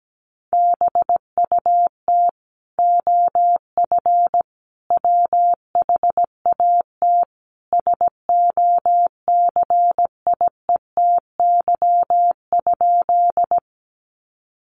Morse code, heard as a keyed tone, about 17 wpm, 700 Hz. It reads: BUT OF WHAT SOCIETY?
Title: Sentences from Top 700 Words 17wpm